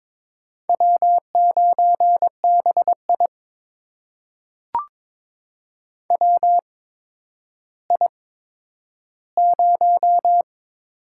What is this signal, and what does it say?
Morse code, heard as a keyed tone, about 22 wpm, spaced as for 11 wpm, 700 Hz. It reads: W9BI WI0
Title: USA Call Signs - 4 Characters - 8x Character Spacing - 22wpm